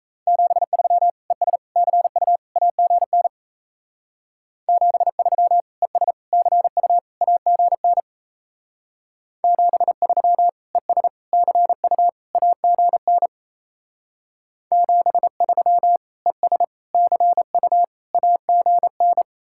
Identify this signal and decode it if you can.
Morse code, heard as a keyed tone, about 34 wpm, 700 Hz. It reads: 73 ES CU AGN 73 ES CU AGN 73 ES CU AGN 73 ES CU AGN
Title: Speed-Racing Common QSO Phrases - 28wpm